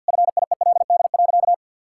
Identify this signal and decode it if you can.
Morse code, heard as a keyed tone, about 48 wpm, 700 Hz. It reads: FIELD.
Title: Sentences from Top 1000 Words - Review Part 5 - 50wpm